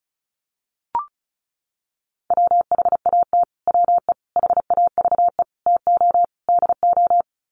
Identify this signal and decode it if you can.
Morse code, heard as a keyed tone, about 34 wpm, 700 Hz. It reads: WHAT WE HAVE TO DO
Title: Sets of 5 Words Encore 35wpm